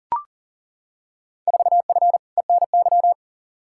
Morse code, heard as a keyed tone, about 40 wpm, 700 Hz. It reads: 4F ENY
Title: Field Day Exchanges - 40wpm